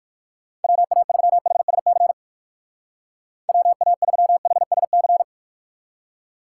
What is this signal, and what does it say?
Morse code, heard as a keyed tone, about 55 wpm, 700 Hz. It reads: WA3HSC WA3HSC
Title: Speed Racing - USA 2x3 Call Signs 45wpm